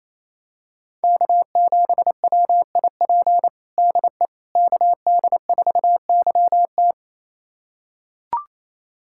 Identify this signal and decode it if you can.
Morse code, heard as a keyed tone, about 28 wpm, 700 Hz. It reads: K7WIP DE KD4YT
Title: USA Callsign DE Callsign - 5 Characters - 28wpm